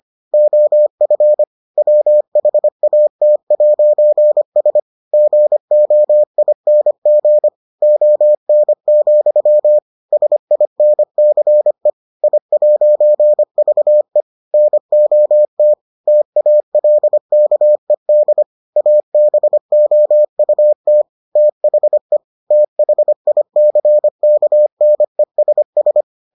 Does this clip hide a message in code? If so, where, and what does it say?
Yes: Morse code from 0.3 to 26.0 s, heard as a keyed tone, about 25 wpm, 600 Hz. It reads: OF WHAT'S GOING ON, SINCE I'VE NOT TALKED ABOUT THE THICKNESS